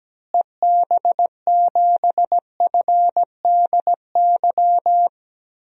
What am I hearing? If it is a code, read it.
Morse code, heard as a keyed tone, about 17 wpm, 700 Hz. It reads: EB7FDY
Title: DXCC-100 Call Signs - 6 Characters - Repeated 3x - 17wpm